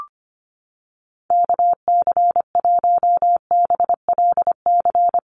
Morse code, heard as a keyed tone, about 25 wpm, 700 Hz. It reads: KC1BLC
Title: USA Call Signs Repeated 3x 25wpm